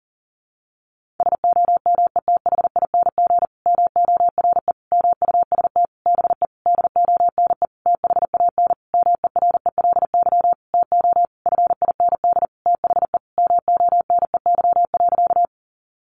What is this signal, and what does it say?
Morse code, heard as a keyed tone, about 40 wpm, 700 Hz. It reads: SOMETHING MORE MUST BE DONE THAN MERELY TO FIND THE MONEY.
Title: Sentences from Top 1400 Words 40wpm